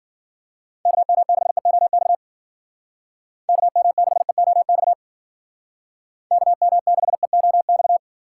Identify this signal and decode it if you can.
Morse code, heard as a keyed tone, about 52 wpm, 700 Hz. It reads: KM6EOX KM6EOX KM6EOX